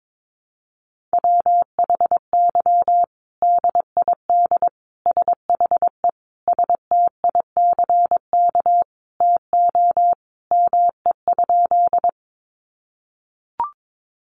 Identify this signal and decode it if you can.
Morse code, heard as a keyed tone, about 22 wpm, 700 Hz. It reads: WHY DID SHE STICK TO ME?
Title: Sentences from Top 1200 Words 22wpm